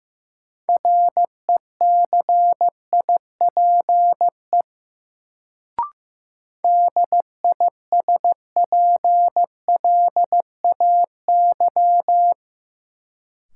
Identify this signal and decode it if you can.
Morse code, heard as a keyed tone, about 15 wpm, 700 Hz. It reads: RECIPE DISPLAY
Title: Sentences from Top 1200 Words - New Word Drill - 15wpm